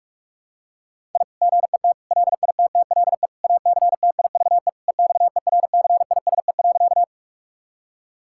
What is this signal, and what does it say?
Morse code, heard as a keyed tone, about 46 wpm, 700 Hz. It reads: I GET LITTLE ACTIVE EXERCISE.
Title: Sentences from Top 1000 Words - Review Part 6 - 45wpm